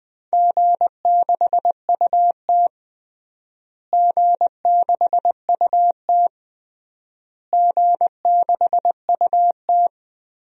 Morse code, heard as a keyed tone, about 20 wpm, 700 Hz. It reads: G6UT G6UT G6UT
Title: DXCC-100 Call Signs - 4 Characters - Repeated 3x - 20wpm